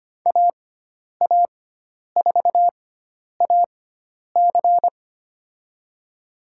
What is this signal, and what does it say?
Morse code, heard as a keyed tone, about 25 wpm, spaced as for 10 wpm, 700 Hz. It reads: AA4AC
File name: Lesson 16g - Letter c - Call Signs 25wpm